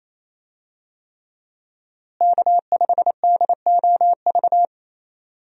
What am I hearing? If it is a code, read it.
Morse code, heard as a keyed tone, about 28 wpm, 700 Hz. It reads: K5DOV
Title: USA Call Signs - 5 Characters - Repeated 3x - 28wpm